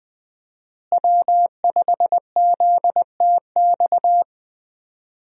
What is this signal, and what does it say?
Morse code, heard as a keyed tone, about 20 wpm, 700 Hz. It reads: W5ZTX
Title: USA Call Signs - 5 Characters - Repeated 2x - 20wpm